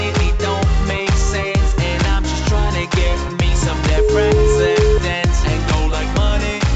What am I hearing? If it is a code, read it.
Ringback tone (Europe, 425 Hz), with noise.